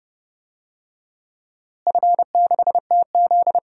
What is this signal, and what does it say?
Morse code, heard as a keyed tone, about 30 wpm, 700 Hz. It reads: F6TZ
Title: DXCC-100 Call Signs - 4 Characters - Repeated 3x - 30wpm